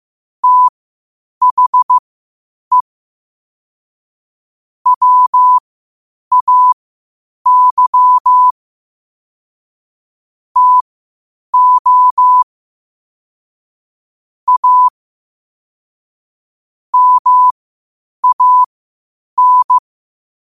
Morse code, heard as a keyed tone, about 15 wpm, spaced as for 8 wpm, 1000 Hz. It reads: THE WAY TO A MAN
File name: qotdfiles / 8wpm